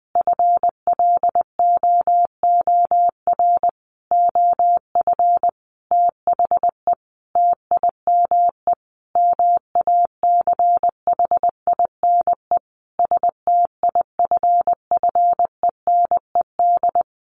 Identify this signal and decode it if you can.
Morse code, heard as a keyed tone, about 20 wpm, 700 Hz. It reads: FLOOR OF THE TIME MACHINE STIFFENED
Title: A Sound Of Thunder 20wpm